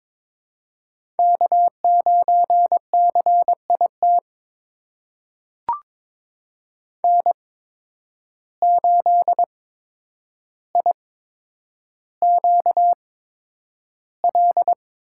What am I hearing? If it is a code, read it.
Morse code, heard as a keyed tone, about 22 wpm, spaced as for 9 wpm, 700 Hz. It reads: K9CIT N8IQL
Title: USA Call Signs - 5 Characters - 8x Character Spacing - 22wpm